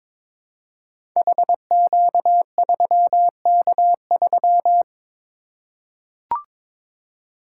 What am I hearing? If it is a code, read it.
Morse code, heard as a keyed tone, about 22 wpm, 700 Hz. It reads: HQ3K3
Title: Code Groups - 5 Letter:Numbers - 22wpm